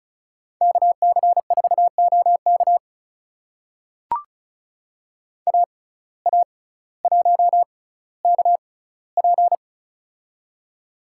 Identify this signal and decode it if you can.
Morse code, heard as a keyed tone, about 36 wpm, spaced as for 18 wpm, 700 Hz. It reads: KC4OK AA1KP
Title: USA Call Signs - 5 Characters - 6x Character Spacing - 35wpm